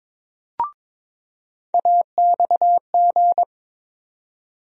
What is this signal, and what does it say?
Morse code, heard as a keyed tone, about 22 wpm, 700 Hz. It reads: AXG
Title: USA Call Sign Postfixes - 3 Letters 22wpm